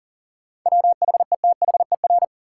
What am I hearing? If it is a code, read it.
Morse code, heard as a keyed tone, about 40 wpm, 700 Hz. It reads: WHETHER
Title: Sets of 3 Words - New Word Drill - 40wpm